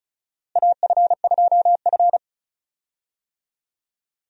Morse code, heard as a keyed tone, about 36 wpm, 700 Hz. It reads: AF2F
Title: USA Call Signs - 4 Characters - Repeated 3x - 35wpm